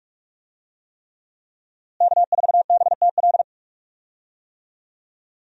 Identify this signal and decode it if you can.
Morse code, heard as a keyed tone, about 43 wpm, 700 Hz. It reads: K4BTL
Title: USA Call Signs - 5 Characters - Repeated 3x - 45wpm